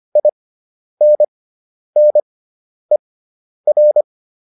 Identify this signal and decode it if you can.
Morse code, heard as a keyed tone, about 25 wpm, spaced as for 10 wpm, 600 Hz. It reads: INNER